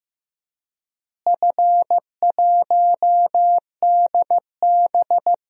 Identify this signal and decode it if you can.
Morse code, heard as a keyed tone, about 16 wpm, 700 Hz. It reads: F1DB
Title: DXCC-100 Call Signs - 4 Characters - Repeated 2x - 15wpm